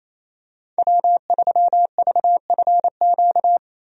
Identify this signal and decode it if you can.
Morse code, heard as a keyed tone, about 27 wpm, 700 Hz. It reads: W3VFQ
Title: USA Call Signs - 5 Characters - 28wpm